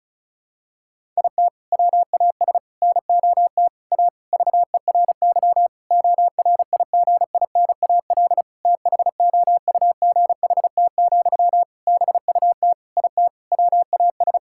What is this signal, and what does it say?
Morse code, heard as a keyed tone, about 34 wpm, 700 Hz. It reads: IT WAS NOT A VERY ORIGINAL THOUGHT, BUT IT WAS
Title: Sentences from Top 1000 Words - Review Part 6 - 35wpm